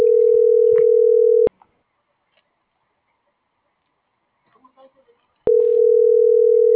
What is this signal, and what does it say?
Ringback tone (North America), 440 + 480 Hz.